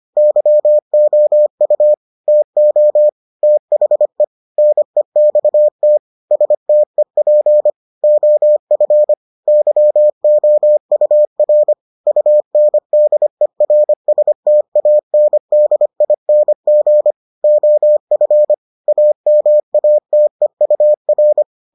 Morse code, heard as a keyed tone, about 25 wpm, 600 Hz. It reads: YOU TO THE NEXT STEP OF YOUR UNDERSTANDING OF AMATEUR